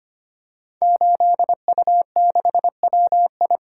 Morse code, heard as a keyed tone, about 25 wpm, 700 Hz. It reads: 8U6WI
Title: Code Groups - 5 Letter:Numbers - 25wpm